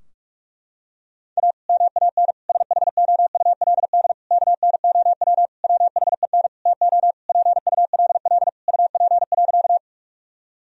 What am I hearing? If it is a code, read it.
Morse code, heard as a keyed tone, about 44 wpm, 700 Hz. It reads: A MAN SHOULD KNOW WHEN TO PULL UP.
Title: Sentences from Top 1000 Words - Review Part 2 - 45wpm